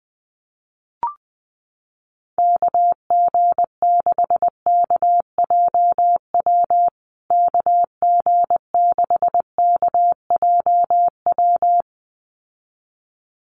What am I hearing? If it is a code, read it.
Morse code, heard as a keyed tone, about 19 wpm, 700 Hz. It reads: KG6KJW KG6KJW
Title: USA Call Signs Repeated 2x 20wpm